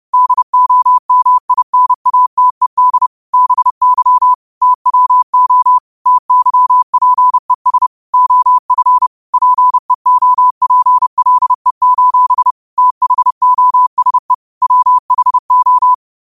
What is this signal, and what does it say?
Morse code, heard as a keyed tone, about 30 wpm, 1000 Hz. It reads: DOMINATED BY TWO TYPES OF PEOPLE: THOSE WHO